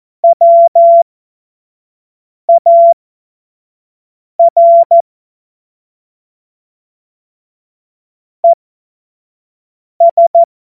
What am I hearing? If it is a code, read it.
Morse code, heard as a keyed tone, about 14 wpm, spaced as for 5 wpm, 680 Hz. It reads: WAR ES
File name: Morsekurs / Witze / Witz 03. Mai 2022 25 F70